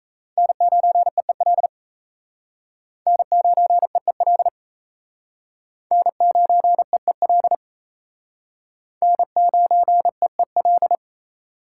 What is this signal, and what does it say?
Morse code, heard as a keyed tone, about 36 wpm, 700 Hz. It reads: N9EEL N9EEL N9EEL N9EEL